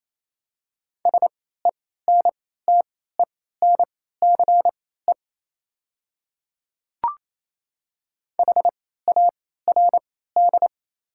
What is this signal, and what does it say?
Morse code, heard as a keyed tone, about 28 wpm, spaced as for 16 wpm, 700 Hz. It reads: SENTENCE HARD
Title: 3x Character Spacing - Top 300 Words - 28wpm